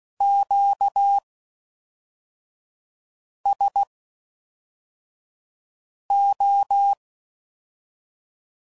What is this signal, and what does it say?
Morse code, heard as a keyed tone, about 16 wpm, spaced as for 4 wpm, 800 Hz. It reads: QSO